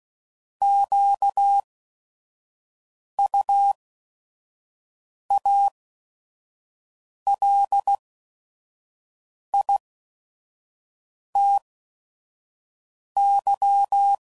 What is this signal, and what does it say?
Morse code, heard as a keyed tone, about 16 wpm, spaced as for 5 wpm, 800 Hz. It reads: QUALITY